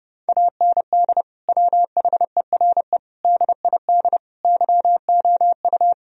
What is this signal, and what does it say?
Morse code, heard as a keyed tone, about 30 wpm, 700 Hz. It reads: AND WHERE DID YOU
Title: Sentences from Top 1000 Words - Review Part 1 - 30wpm